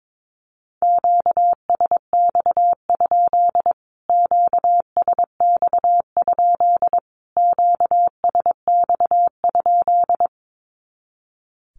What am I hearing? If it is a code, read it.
Morse code, heard as a keyed tone, about 22 wpm, 700 Hz. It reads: QSX? QSX? QSX?
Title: Q-Codes Repeated 3x 22wpm